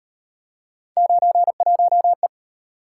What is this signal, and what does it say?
Morse code, heard as a keyed tone, about 37 wpm, 700 Hz. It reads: 91E